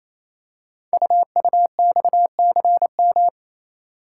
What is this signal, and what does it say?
Morse code, heard as a keyed tone, about 27 wpm, 700 Hz. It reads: UUXCM
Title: Code Groups - 5 Letters - 28wpm